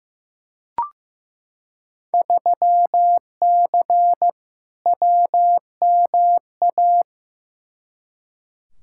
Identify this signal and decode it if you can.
Morse code, heard as a keyed tone, about 15 wpm, 700 Hz. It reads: 3C WMA
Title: Field Day Exchanges Encore - 15wpm